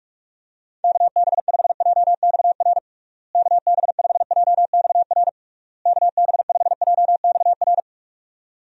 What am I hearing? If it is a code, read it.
Morse code, heard as a keyed tone, about 42 wpm, 700 Hz. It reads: KB5JXR KB5JXR KB5JXR
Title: USA Call Signs - 6 Characters - Repeated 3x - 45wpm